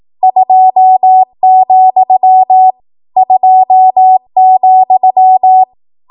Morse code, heard as a keyed tone, about 18 wpm, 750 Hz. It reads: 2, 2,
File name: Morse / 18WPM